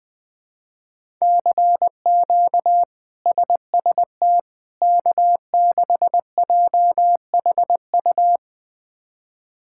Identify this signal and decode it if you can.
Morse code, heard as a keyed tone, about 20 wpm, 700 Hz. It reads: CQ SST K6JHU
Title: K1USN SST QSO Practice - 20wpm